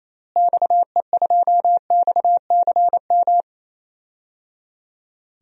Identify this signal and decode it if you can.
Morse code, heard as a keyed tone, about 28 wpm, 700 Hz. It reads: XE2XCM
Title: DXCC-100 Call Signs - 6 Characters - Repeated 3x - 28wpm